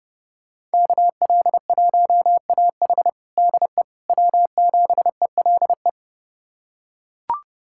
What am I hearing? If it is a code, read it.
Morse code, heard as a keyed tone, about 30 wpm, 700 Hz. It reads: KL1AH DE W7ELE
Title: USA Callsign DE Callsign 30wpm